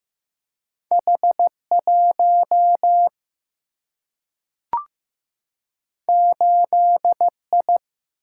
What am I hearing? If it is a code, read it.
Morse code, heard as a keyed tone, about 15 wpm, 700 Hz. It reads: H1 8I
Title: Single-Letter Single-Number 15wpm